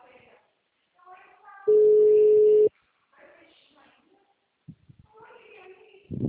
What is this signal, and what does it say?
Ringback tone (Europe, 425 Hz), with speech.
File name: out- -2029- - - .8780